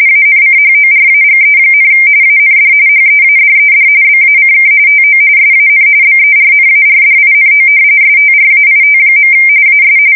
RTTY, about 45.45 baud. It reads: ANCHORAGE, AND HEADED BACK TO SAN BERNARDINO STRAIT. AS
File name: fsk170 / battleshipnj